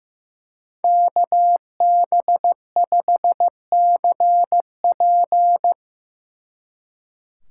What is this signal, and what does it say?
Morse code, heard as a keyed tone, about 17 wpm, 700 Hz. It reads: KB5CP
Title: Speed Racing - USA 2x2 Call Signs 15wpm